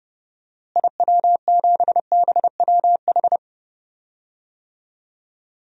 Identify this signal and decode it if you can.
Morse code, heard as a keyed tone, about 32 wpm, 700 Hz. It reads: IW7BWH
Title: DXCC-100 Call Signs - 6 Characters - Repeated 2x - 30wpm